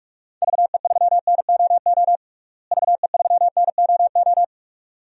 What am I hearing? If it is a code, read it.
Morse code, heard as a keyed tone, about 46 wpm, 700 Hz. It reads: VE3NOO VE3NOO
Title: DXCC-100 Call Signs - 6 Characters - Repeated 2x - 45wpm